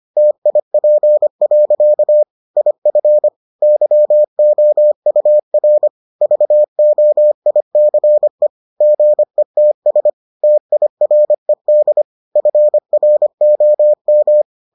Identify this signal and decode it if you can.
Morse code, heard as a keyed tone, about 25 wpm, 600 Hz. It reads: TIP. IF YOUR VOICE GETS TIRED FROM